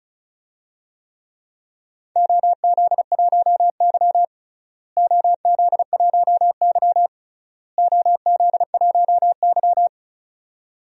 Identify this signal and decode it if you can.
Morse code, heard as a keyed tone, about 36 wpm, 690 Hz. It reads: OZ1Y OZ1Y OZ1Y
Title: DXCC-100 Call Signs - 4 Characters - Repeated 3x - 35wpm